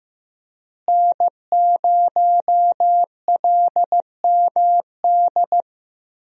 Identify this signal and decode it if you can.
Morse code, heard as a keyed tone, about 15 wpm, 700 Hz. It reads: N0LMD
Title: USA Call Signs - 5 Characters - Repeated 2x - 15wpm